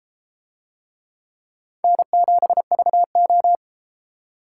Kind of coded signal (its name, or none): Morse code